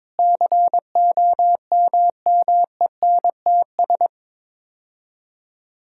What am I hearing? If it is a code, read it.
Morse code, heard as a keyed tone, about 22 wpm, 700 Hz. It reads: COMMENTS
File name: Sets of 9 Words - 22wpm